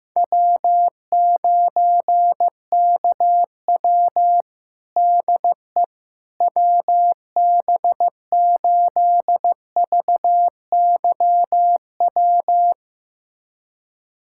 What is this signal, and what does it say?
Morse code, heard as a keyed tone, about 16 wpm, 700 Hz. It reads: W9KW DE WB8VYW
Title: USA Callsign DE Callsign 15wpm